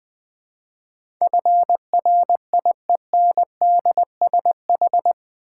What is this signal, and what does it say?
Morse code, heard as a keyed tone, about 20 wpm, 700 Hz. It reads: FRIENDSH
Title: Build A Word - 4 Letters At A Time - 10-Letter Words - 20wpm